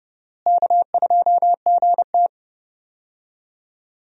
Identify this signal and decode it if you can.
Morse code, heard as a keyed tone, about 32 wpm, 700 Hz. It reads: K2GT
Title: USA Call Signs - 4 Characters - Repeated 3x - 30wpm